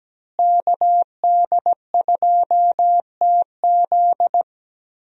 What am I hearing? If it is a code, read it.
Morse code, heard as a keyed tone, about 17 wpm, 700 Hz. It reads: KD2TZ
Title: USA Call Signs - 5 Characters - Repeated 3x - 17wpm